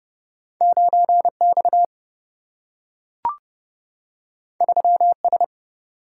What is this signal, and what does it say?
Morse code, heard as a keyed tone, about 30 wpm, 700 Hz. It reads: 9 X 3 S
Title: Single-Letter Single-Number 30wpm